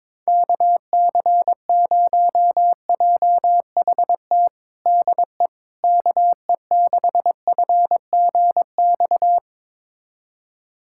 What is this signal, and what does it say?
Morse code, heard as a keyed tone, about 22 wpm, 700 Hz. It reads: KC0JHT DE KE6FGX
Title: USA Callsign DE Callsign - 6 Characters - 22wpm